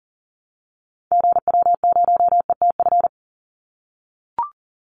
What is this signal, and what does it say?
Morse code, heard as a keyed tone, about 40 wpm, 700 Hz. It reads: GW0ETF
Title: Call Signs - 40wpm